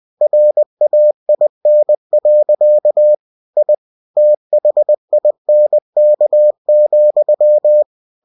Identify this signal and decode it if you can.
Morse code, heard as a keyed tone, about 20 wpm, 600 Hz. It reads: RAIN. I THINK,